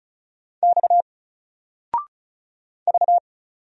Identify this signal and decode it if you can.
Morse code, heard as a keyed tone, about 36 wpm, 700 Hz. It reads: X V
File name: Single-Letter-Number-35wpm